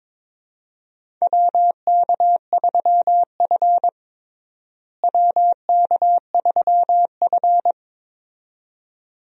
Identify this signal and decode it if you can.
Morse code, heard as a keyed tone, about 22 wpm, 700 Hz. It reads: WK3F WK3F
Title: USA Call Signs - 4 Characters - Repeated 2x - 22wpm